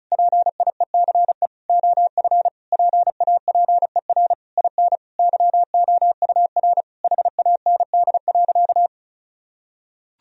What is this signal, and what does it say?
Morse code, heard as a keyed tone, about 34 wpm, 700 Hz. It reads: PIECE OF PAPER IN YOUR HAND.
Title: Sentences from Top 1000 Words - Review Part 2 - 35wpm